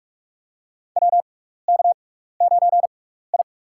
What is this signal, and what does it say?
Morse code, heard as a keyed tone, about 43 wpm, spaced as for 15 wpm, 700 Hz. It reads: WK9I
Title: USA Call Signs - 4 Characters - 6x Character Spacing - 45wpm